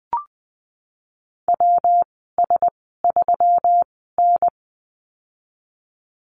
Morse code, heard as a keyed tone, about 20 wpm, spaced as for 15 wpm, 700 Hz. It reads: WS3N
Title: 128 Lesson 21g - Number 6 - Call Signs 20wpm